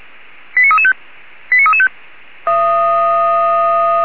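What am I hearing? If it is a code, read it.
Five-tone selective call (ZVEI), heard as quick runs of five tones, tones of about 70 ms. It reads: 89335 89335
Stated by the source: ZVEI five-tone selective call received on an SDR.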